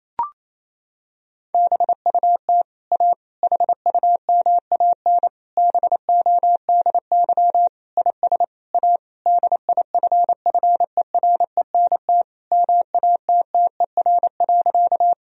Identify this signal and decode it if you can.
Morse code, heard as a keyed tone, about 28 wpm, 700 Hz. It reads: BUT A HUMAN BODY IS A DIFFERENT MATTER.
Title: Sentences from Top 1000 Words - Review Part 3 - 28wpm